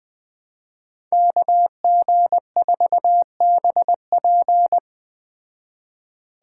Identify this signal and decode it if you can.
Morse code, heard as a keyed tone, about 20 wpm, 700 Hz. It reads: KG4BP
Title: Speed Racing - USA 2x2 Call Signs 17wpm